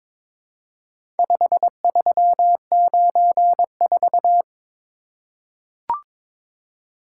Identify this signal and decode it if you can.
Morse code, heard as a keyed tone, about 22 wpm, 700 Hz. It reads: 5394
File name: practice / 22wpm / Four Letter-Number Combinations - 22wpm